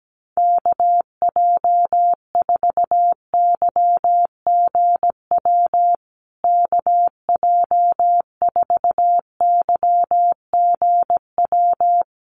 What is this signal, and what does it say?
Morse code, heard as a keyed tone, about 17 wpm, 700 Hz. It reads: KJ4YGW KJ4YGW
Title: USA Call Signs Repeated 3x 17wpm